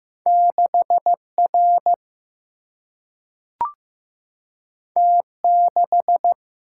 Morse code, heard as a keyed tone, about 15 wpm, 700 Hz. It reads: 6R T6
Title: Single-Letter Single-Number 15wpm